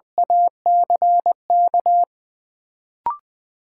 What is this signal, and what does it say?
Morse code, heard as a keyed tone, about 21 wpm, 700 Hz. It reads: ACK
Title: Sentences from Top 1100 Words - New Word Drill Alt - 20wpm